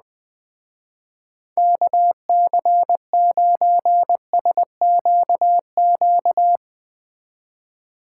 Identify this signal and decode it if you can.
Morse code, heard as a keyed tone, about 20 wpm, 700 Hz. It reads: KC9SQQ
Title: Speed Racing - USA 2x3 Call Signs 17wpm